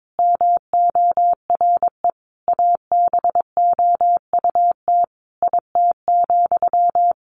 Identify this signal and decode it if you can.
Morse code, heard as a keyed tone, about 22 wpm, 700 Hz. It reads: MORE ABOUT IT,
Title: Sentences from Top 200 Words 22wpm